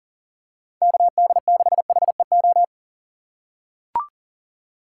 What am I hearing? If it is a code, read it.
Morse code, heard as a keyed tone, about 40 wpm, 700 Hz. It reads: KD6HEO
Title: USA Call Signs - 4x Character Spacing - 40wpm